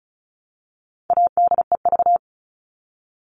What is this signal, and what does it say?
Morse code, heard as a keyed tone, about 34 wpm, 700 Hz. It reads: ADEV
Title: All Cut Numbers - 4-Digit 35wpm